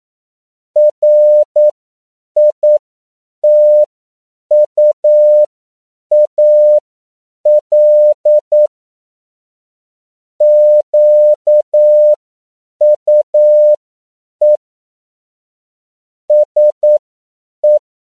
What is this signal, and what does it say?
Morse code, heard as a keyed tone, about 9 wpm, spaced as for 7 wpm, 600 Hz. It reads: RITUAL QUE SE